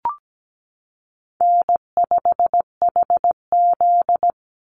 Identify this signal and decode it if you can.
Morse code, heard as a keyed tone, about 17 wpm, 700 Hz. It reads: N5HZ
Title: Call Signs - 17wpm